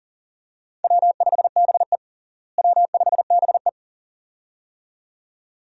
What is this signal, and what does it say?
Morse code, heard as a keyed tone, about 40 wpm, 700 Hz. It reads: W5BE W5BE
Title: USA Call Signs - 4 Characters - Repeated 2x - 40wpm